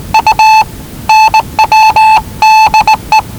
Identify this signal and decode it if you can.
Morse code, heard as a keyed tone, about 19 wpm, 870 Hz. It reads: U NWDE
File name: WellingtonRadioZlwCloseDownOnHfAndMf / 01-Zlw 4277khz 417.5 qrt until 0925z